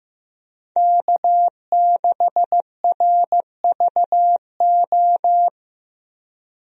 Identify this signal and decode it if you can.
Morse code, heard as a keyed tone, about 15 wpm, 700 Hz. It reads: K6RVO
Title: USA Call Signs - 5 Characters - Repeated 3x - 15wpm